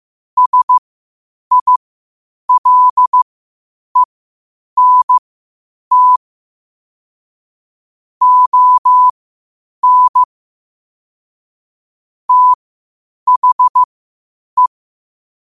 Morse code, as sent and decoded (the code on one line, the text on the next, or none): ... .. .-.. -. - / --- -. / - .... .
SILENT ON THE